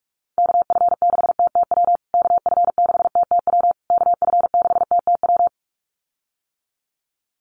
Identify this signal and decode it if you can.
Morse code, heard as a keyed tone, about 46 wpm, 690 Hz. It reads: KF6TTW KF6TTW KF6TTW
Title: USA Call Signs Repeated 3x 45wpm